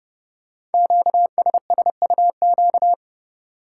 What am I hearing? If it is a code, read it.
Morse code, heard as a keyed tone, about 30 wpm, 690 Hz. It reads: QSSUQ
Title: Code Groups - 5 Letters - 30wpm